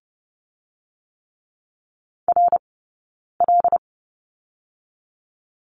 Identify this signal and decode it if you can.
Morse code, heard as a keyed tone, about 30 wpm, 700 Hz. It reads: R L
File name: course / 30:10wpm / 086 Lesson 15d - Number 5 - Two Characters 30wpm